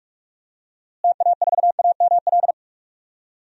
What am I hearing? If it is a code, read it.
Morse code, heard as a keyed tone, about 46 wpm, 700 Hz. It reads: TA4AML
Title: DXCC-100 Call Signs - 6 Characters - Repeated 3x - 45wpm